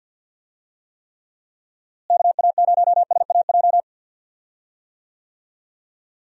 Morse code, heard as a keyed tone, about 47 wpm, 700 Hz. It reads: KA0SAJ
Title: USA Call Signs - 6 Characters - Repeated 3x - 50wpm